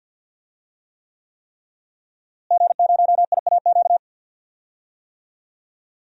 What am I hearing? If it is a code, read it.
Morse code, heard as a keyed tone, about 51 wpm, 700 Hz. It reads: G0IAQ
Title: DXCC-100 Call Signs - 5 Characters - Repeated 3x - 50wpm